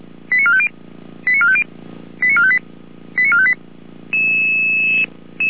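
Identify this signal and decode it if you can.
Five-tone selective call (ZVEI), heard as quick runs of five tones, tones of about 70 ms. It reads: 89450 89450 89458 89458
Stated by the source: ZVEI five-tone selective call 89457 received by SDR.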